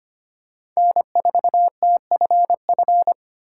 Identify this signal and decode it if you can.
Morse code, heard as a keyed tone, about 25 wpm, 700 Hz. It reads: N4TFF
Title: USA Call Signs - 3x Character Spacing - 25wpm